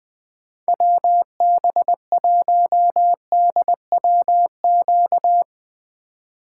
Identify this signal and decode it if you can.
Morse code, heard as a keyed tone, about 20 wpm, 700 Hz. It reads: WB1DWQ
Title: USA Call Signs - 6 Characters - 2x Character Spacing - 20wpm